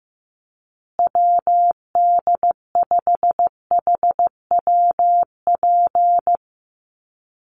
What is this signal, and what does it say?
Morse code, heard as a keyed tone, about 15 wpm, 700 Hz. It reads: WD5HWP
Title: USA Call Signs Repeated 2x 15wpm